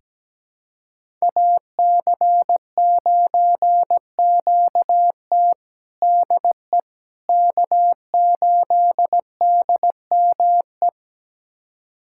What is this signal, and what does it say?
Morse code, heard as a keyed tone, about 17 wpm, 700 Hz. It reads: AC9QT DE K8DME
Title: USA Callsign DE Callsign - 5 Characters - 17wpm